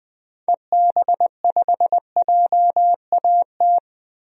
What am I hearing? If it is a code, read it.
Morse code, heard as a keyed tone, about 20 wpm, 700 Hz. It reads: EB5JAT
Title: DXCC-100 Call Signs - 6 Characters - 20wpm